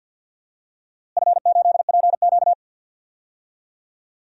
Morse code, heard as a keyed tone, about 47 wpm, 700 Hz. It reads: F8PQ
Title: DXCC-100 Call Signs - 4 Characters - Repeated 3x - 50wpm